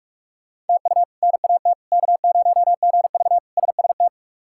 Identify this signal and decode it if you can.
Morse code, heard as a keyed tone, about 43 wpm, 700 Hz. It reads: TU NAT K0GV SST
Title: K1USN SST QSO Practice - 45wpm